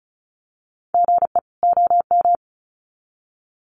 Morse code, heard as a keyed tone, about 34 wpm, 700 Hz. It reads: GE OM
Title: Common QSO Phrases 35wpm